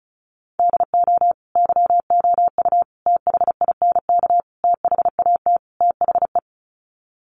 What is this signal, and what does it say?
Morse code, heard as a keyed tone, about 34 wpm, 700 Hz. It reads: DO YOU THINK THAT THE
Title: Sets of 5 Words Encore 35wpm